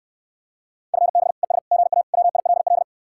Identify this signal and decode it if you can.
Morse code, heard as a keyed tone, about 69 wpm, 700 Hz. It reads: FB ES GA WILL